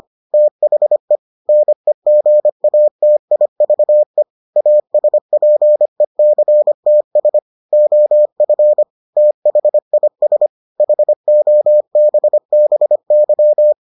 Morse code, heard as a keyed tone, about 25 wpm, 600 Hz. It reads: THE NEGATIVE ASPECTS OF THIS HOBBY